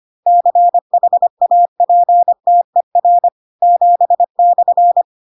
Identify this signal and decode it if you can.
Morse code, heard as a keyed tone, about 25 wpm, 700 Hz. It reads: CHAPTER 7/
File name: Pinocchio / Pinocchio ch7 25 25 ik1hge